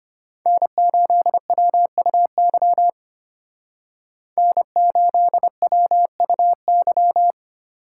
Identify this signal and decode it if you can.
Morse code, heard as a keyed tone, about 32 wpm, 700 Hz. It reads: N8WUY N8WUY